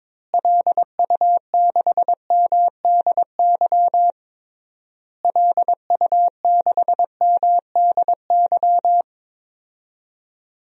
Morse code, heard as a keyed tone, about 22 wpm, 700 Hz. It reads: LU6MDY LU6MDY
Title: DXCC-100 Call Signs - 6 Characters - Repeated 3x - 22wpm